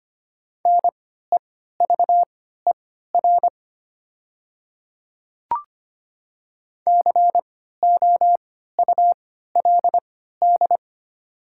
Morse code, heard as a keyed tone, about 25 wpm, spaced as for 14 wpm, 700 Hz. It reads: NEVER COULD
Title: 3x Character Spacing - Top 300 Words - 25wpm